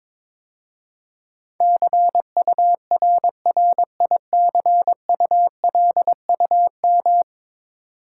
Morse code, heard as a keyed tone, about 21 wpm, 700 Hz. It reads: CURRICULUM
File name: 10 Letter Words - First 3 Letters Given - 22wpm